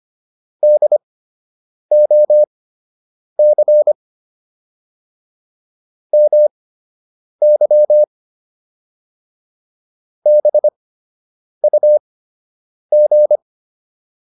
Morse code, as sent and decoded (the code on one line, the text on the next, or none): -.. --- -.-. / -- -.-- / -... ..- --.
DOC MY BUG